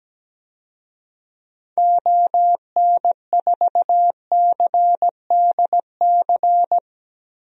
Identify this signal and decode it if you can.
Morse code, heard as a keyed tone, about 17 wpm, 700 Hz. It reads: ON4CDC
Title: DXCC-100 Call Signs - 6 Characters - Repeated 3x - 17wpm